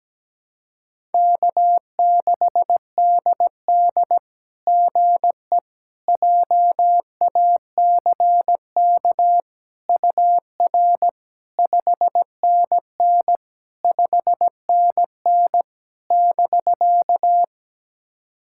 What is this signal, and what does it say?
Morse code, heard as a keyed tone, about 17 wpm, 700 Hz. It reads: K6DD GE JACK UR 5NN 5NN <BK>
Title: SOTA QSO Practice 17wpm